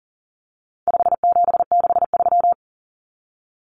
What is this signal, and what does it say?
Morse code, heard as a keyed tone, about 40 wpm, 700 Hz. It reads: 5763
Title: Numbers - 4-Digit 40wpm